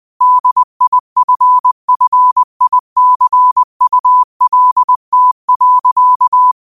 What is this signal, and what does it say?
Morse code, heard as a keyed tone, about 20 wpm, 1000 Hz. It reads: DIFFICULT.
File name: qotdfiles / 20wpm